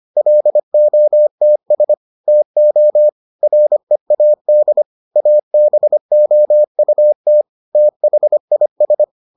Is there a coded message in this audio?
Yes: Morse code from 0.2 to 9.0 s, heard as a keyed tone, about 25 wpm, 600 Hz. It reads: LOTS TO READ ABOUT THIS